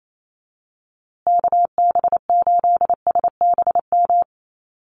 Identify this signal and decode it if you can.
Morse code, heard as a keyed tone, about 28 wpm, 700 Hz. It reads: KB8SBM
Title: USA Call Signs Repeated 2x 28wpm